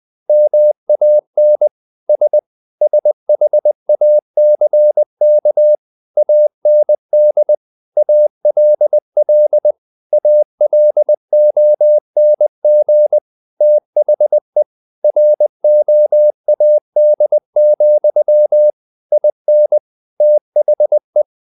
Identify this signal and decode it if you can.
Morse code, heard as a keyed tone, about 20 wpm, 600 Hz. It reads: MAN S SHACK AND ALL ALONG THE ROAD, IN THE
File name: the old man and the sea / the old man and the sea part 1 20wpm